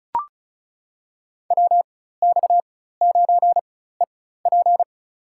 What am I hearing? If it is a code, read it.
Morse code, heard as a keyed tone, about 36 wpm, spaced as for 16 wpm, 700 Hz. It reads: WX9EP
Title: USA Call Signs - 5 Characters - 4x Character Spacing - 35wpm